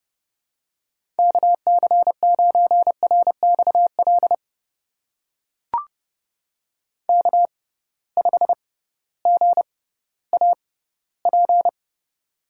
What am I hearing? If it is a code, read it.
Morse code, heard as a keyed tone, about 32 wpm, spaced as for 17 wpm, 700 Hz. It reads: KC9RXL K5GAP
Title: USA Call Signs - 6x Character Spacing - 30wpm